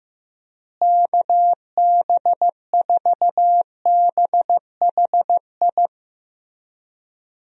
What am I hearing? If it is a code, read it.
Morse code, heard as a keyed tone, about 15 wpm, 700 Hz. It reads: KB4BHI
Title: USA Call Signs - 6 Characters - Repeated 2x - 15wpm